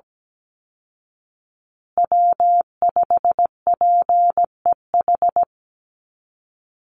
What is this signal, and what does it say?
Morse code, heard as a keyed tone, about 17 wpm, 700 Hz. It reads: W5PEH
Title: Call Signs Repeated 2x - 17wpm